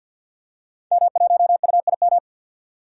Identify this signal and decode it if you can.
Morse code, heard as a keyed tone, about 46 wpm, 700 Hz. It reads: M1UIM
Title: Code Groups - 5 Letter:Numbers - 50wpm